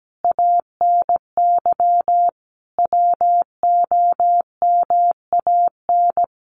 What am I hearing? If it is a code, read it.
Morse code, heard as a keyed tone, about 17 wpm, 700 Hz. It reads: ANY WOMAN
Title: Sentences from Top 1200 Words 17wpm